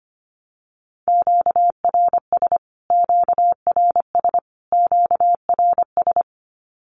Morse code, heard as a keyed tone, about 25 wpm, 700 Hz. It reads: QRS QRS QRS
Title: Q-Codes Repeated 3x 25wpm